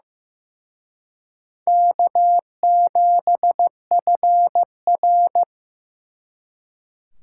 Morse code, heard as a keyed tone, about 18 wpm, 700 Hz. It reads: K7FR
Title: Speed Racing - USA 1x2 Call Signs 15wpm